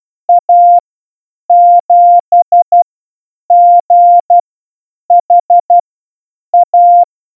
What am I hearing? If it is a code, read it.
Morse code, heard as a keyed tone, about 12 wpm, spaced as for 8 wpm, 700 Hz. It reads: A7GHA